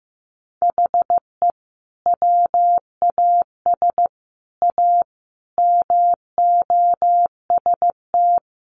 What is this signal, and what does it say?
Morse code, heard as a keyed tone, about 15 wpm, 700 Hz. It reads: HE WAS A MOST
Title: Sentences from Top 1200 Words 15wpm